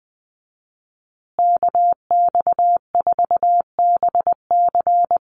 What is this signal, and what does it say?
Morse code, heard as a keyed tone, about 20 wpm, 700 Hz. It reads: KX4BC
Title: USA Call Signs Repeated 2x 20wpm